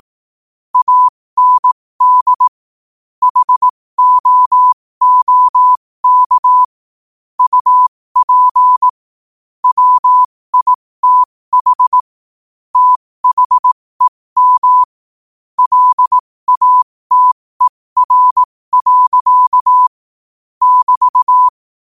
Morse code, heard as a keyed tone, about 18 wpm, spaced as for 15 wpm, 1000 Hz. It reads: AND HOOK UP WITH THEM LATER. <BT>
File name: qotdfiles / 15wpm